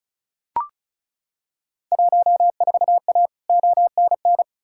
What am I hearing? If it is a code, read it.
Morse code, heard as a keyed tone, about 36 wpm, 700 Hz. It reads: 14A ONN
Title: Field Day Exchanges Encore - 35wpm